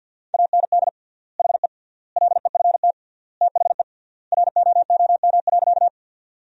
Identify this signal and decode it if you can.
Morse code, heard as a keyed tone, about 49 wpm, 700 Hz. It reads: AND HE LEFT THE ROOM.
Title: 3x Word-Spacing - Sentences from Top 300 Words - 50wpm